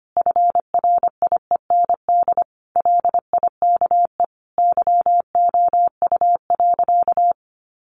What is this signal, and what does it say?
Morse code, heard as a keyed tone, about 25 wpm, 700 Hz. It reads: FRIEND LIKE YOU.
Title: Sentences from Top 1100 Words 25wpm